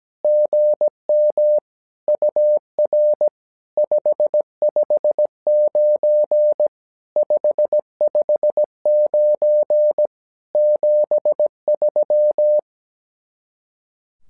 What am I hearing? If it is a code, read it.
Morse code, heard as a keyed tone, about 17 wpm, 600 Hz. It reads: GM UR 559 559 73
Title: SOTA QSO Practice 17wpm